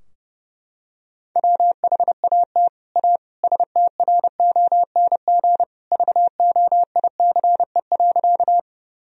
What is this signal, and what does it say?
Morse code, heard as a keyed tone, about 30 wpm, 700 Hz. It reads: WHAT A STRONG VOICE.
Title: Sentences from Top 1000 Words - Review Part 3 - 30wpm